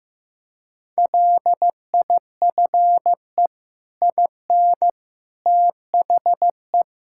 Morse code, heard as a keyed tone, about 16 wpm, 700 Hz. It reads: LIFE IN THE
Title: Sentences from Top 1000 Words - Review Part 5 - 15wpm